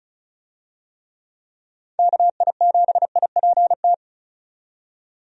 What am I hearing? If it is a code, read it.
Morse code, heard as a keyed tone, about 34 wpm, 700 Hz. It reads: KI7IPT
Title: USA Call Signs - 6 Characters - 35wpm